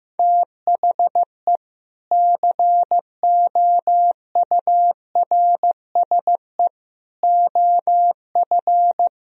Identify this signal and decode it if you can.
Morse code, heard as a keyed tone, about 15 wpm, 700 Hz. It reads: THE COURSE OF
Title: Speed-Racing - Sets of 4 Words 15wpm